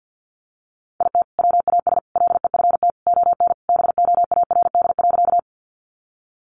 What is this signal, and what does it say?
Morse code, heard as a keyed tone, about 48 wpm, 700 Hz. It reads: IT WAS LEFT ON BOARD.
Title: Sentences from Top 700 Words 50wpm